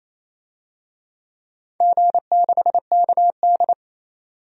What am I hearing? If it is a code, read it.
Morse code, heard as a keyed tone, about 28 wpm, 700 Hz. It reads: G6KD
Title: DXCC-100 Call Signs - 4 Characters - 28wpm